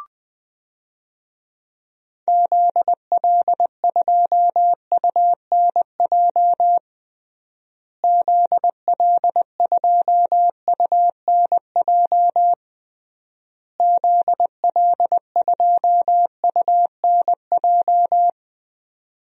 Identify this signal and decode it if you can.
Morse code, heard as a keyed tone, about 20 wpm, 700 Hz. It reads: ZL2UNJ ZL2UNJ ZL2UNJ
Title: DXCC-100 Call Signs - 6 Characters - Repeated 3x - 20wpm